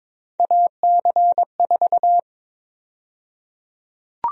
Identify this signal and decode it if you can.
Morse code, heard as a keyed tone, about 22 wpm, 700 Hz. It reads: AC4
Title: USA Call Sign Prefixes - Repeated 2x - 22wpm